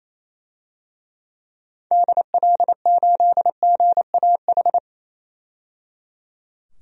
Morse code, heard as a keyed tone, about 27 wpm, 700 Hz. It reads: DL8GAH
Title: DXCC-100 Call Signs - 6 Characters - 28wpm